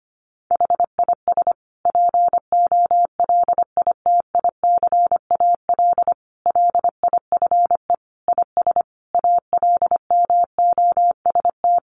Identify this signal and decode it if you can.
Morse code, heard as a keyed tone, about 25 wpm, 700 Hz. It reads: HIS POLITICAL LIFE IS ALMOST
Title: Sentences from Top 1500 Words 25wpm